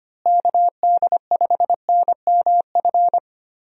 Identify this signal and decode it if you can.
Morse code, heard as a keyed tone, about 25 wpm, 700 Hz. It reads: KD5NMF
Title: USA Call Signs - 6 Characters - Repeated 3x - 25wpm